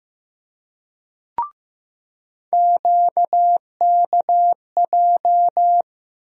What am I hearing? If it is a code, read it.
Morse code, heard as a keyed tone, about 15 wpm, 700 Hz. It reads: QKJ
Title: USA Call Sign Postfixes - 3 Letters 15wpm